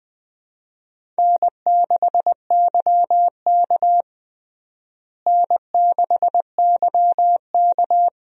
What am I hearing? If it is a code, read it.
Morse code, heard as a keyed tone, about 20 wpm, 700 Hz. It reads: N6YK N6YK
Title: USA Call Signs - 4 Characters - Repeated 3x - 20wpm